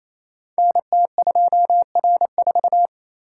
Morse code, heard as a keyed tone, about 29 wpm, 700 Hz. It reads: NT2R4
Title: Code Groups - 5 Letter:Numbers - 28wpm